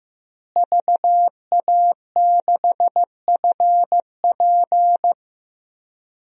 Morse code, heard as a keyed tone, about 15 wpm, 700 Hz. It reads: VA6FP
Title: DXCC-100 Call Signs - 5 Characters - Repeated 2x - 15wpm